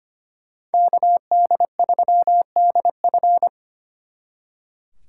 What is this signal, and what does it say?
Morse code, heard as a keyed tone, about 30 wpm, 700 Hz. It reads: KD3DF
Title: Speed Racing - USA 2x2 Call Signs 25wpm